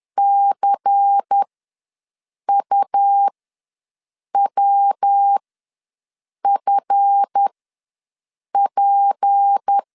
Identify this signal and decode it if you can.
Morse code, heard as a keyed tone, about 10 wpm, spaced as for 6 wpm, 800 Hz. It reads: CUWFP
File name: LEZIONI / 7 GF9Z / 7 GF ws 10wpm 3918